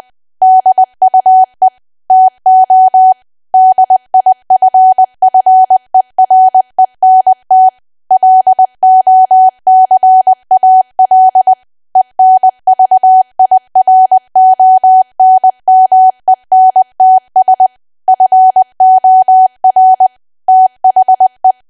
Morse code, heard as a keyed tone, about 20 wpm, 750 Hz. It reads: DUE TO DIFFERENT LOCAL ENVIRONMENTS FOR THE
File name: cwa / W1AW-Level3 / 20WPM